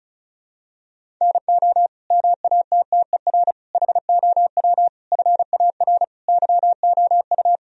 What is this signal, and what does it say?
Morse code, heard as a keyed tone, about 36 wpm, 700 Hz. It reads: NO MATTER HOW FAR YOU
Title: Sentences from Top 1000 Words - Review Part 1 - 35wpm